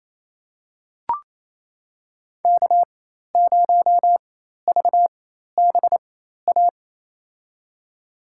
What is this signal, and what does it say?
Morse code, heard as a keyed tone, about 28 wpm, spaced as for 13 wpm, 700 Hz. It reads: K0VBA
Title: USA Call Signs - 4x Character Spacing - 28wpm